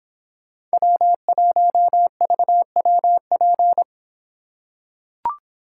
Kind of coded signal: Morse code

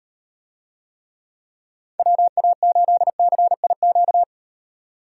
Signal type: Morse code

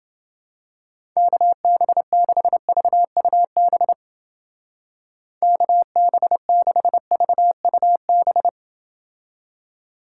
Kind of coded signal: Morse code

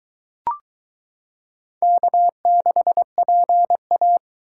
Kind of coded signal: Morse code

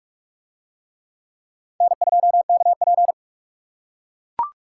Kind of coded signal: Morse code

